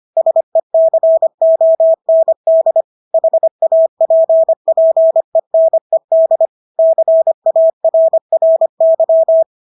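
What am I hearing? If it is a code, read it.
Morse code, heard as a keyed tone, about 25 wpm, 650 Hz. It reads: SECOND HAPPENED CARRY
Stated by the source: Morse practice of English words at 25 wpm from CW med Gnister.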